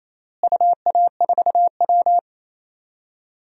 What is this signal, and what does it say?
Morse code, heard as a keyed tone, about 28 wpm, 700 Hz. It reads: UA4W
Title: DXCC-100 Call Signs - 4 Characters - Repeated 2x - 28wpm